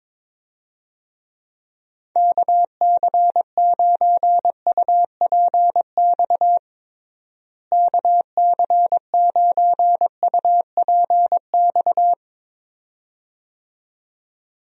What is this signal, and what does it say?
Morse code, heard as a keyed tone, about 22 wpm, 700 Hz. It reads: KC9UPX KC9UPX
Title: USA Call Signs - 6 Characters - Repeated 2x - 22wpm